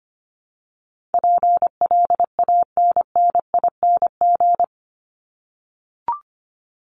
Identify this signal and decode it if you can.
Morse code, heard as a keyed tone, about 25 wpm, 700 Hz. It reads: PLANNING
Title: ING Suffix - Top 100 Words with Base Words 25wpm